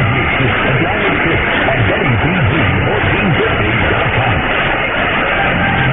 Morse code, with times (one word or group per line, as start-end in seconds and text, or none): none